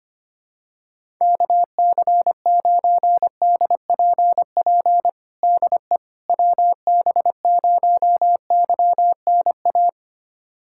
Morse code, heard as a keyed tone, about 25 wpm, 700 Hz. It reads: KC9DPP DE WB0YNA
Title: USA Callsign DE Callsign - 6 Characters - 25wpm